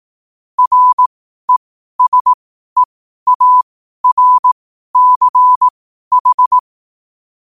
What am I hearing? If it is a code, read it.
Morse code, heard as a keyed tone, about 18 wpm, spaced as for 12 wpm, 1000 Hz. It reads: RESEARCH
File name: qotdfiles / 12wpm